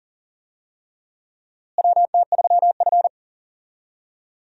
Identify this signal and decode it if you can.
Morse code, heard as a keyed tone, about 40 wpm, 700 Hz. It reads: WT3F
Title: USA Call Signs - 4 Characters - Repeated 3x - 40wpm